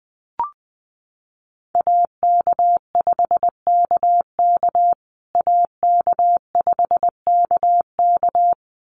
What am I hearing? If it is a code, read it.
Morse code, heard as a keyed tone, about 20 wpm, 700 Hz. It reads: AK5KK AK5KK
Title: USA Call Signs Repeated 2x 20wpm